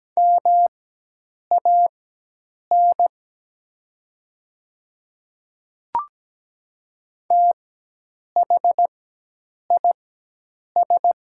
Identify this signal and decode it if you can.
Morse code, heard as a keyed tone, about 17 wpm, spaced as for 8 wpm, 700 Hz. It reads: MAN THIS
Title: 4x Character Spacing - Top 300 Words - 17wpm